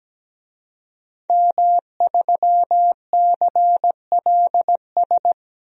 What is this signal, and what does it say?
Morse code, heard as a keyed tone, about 17 wpm, 700 Hz. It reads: M3CLS
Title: DXCC-100 Call Signs - 5 Characters - 17wpm